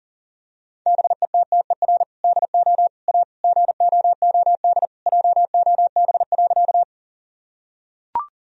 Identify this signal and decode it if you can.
Morse code, heard as a keyed tone, about 40 wpm, 700 Hz. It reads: BETTER DO A GOOD JOB.
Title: Sentences from Top 1000 Words - Review Part 3 - 40wpm